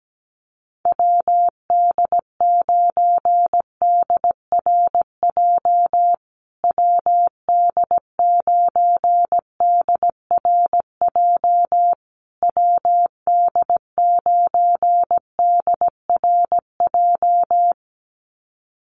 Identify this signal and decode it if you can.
Morse code, heard as a keyed tone, about 17 wpm, 700 Hz. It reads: WD9DRJ WD9DRJ WD9DRJ
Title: USA Call Signs Repeated 3x 17wpm